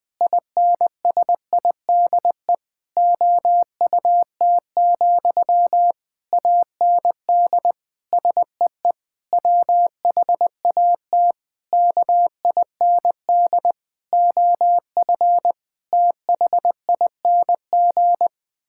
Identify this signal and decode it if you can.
Morse code, heard as a keyed tone, about 20 wpm, 700 Hz. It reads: INSIDE OUT, AND SEE WHAT KIND OF THING
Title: Sentences from Top 1000 Words - Review Part 6 - 20wpm